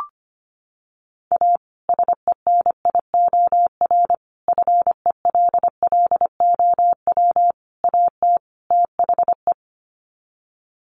Morse code, heard as a keyed tone, about 25 wpm, 700 Hz. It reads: A SENIOR FELLOW AT THE
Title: Sets of 5 Words Encore 25wpm